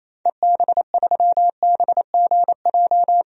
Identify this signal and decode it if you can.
Morse code, heard as a keyed tone, about 28 wpm, 700 Hz. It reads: EB3BGJ
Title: DXCC-100 Call Signs - 6 Characters - Repeated 3x - 28wpm